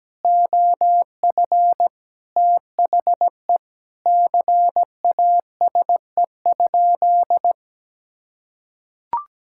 Morse code, heard as a keyed tone, about 17 wpm, 700 Hz. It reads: OF THE CASE?
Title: Sentences from Top 1000 Words - Review Part 5 - 17wpm